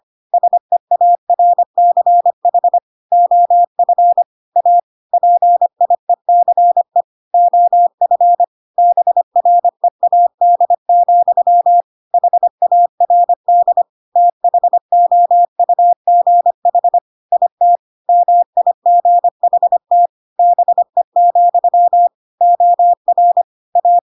Morse code, heard as a keyed tone, about 25 wpm, 700 Hz. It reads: SEARCH OF A PIECE OF BREAD, HARD THOUGH IT MIGHT BE, OR A
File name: Pinocchio / Pinocchio ch5 25 25 ik1hge